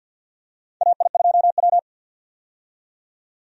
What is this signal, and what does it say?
Morse code, heard as a keyed tone, about 51 wpm, 700 Hz. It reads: AI2W
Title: USA Call Signs - 4 Characters - 50wpm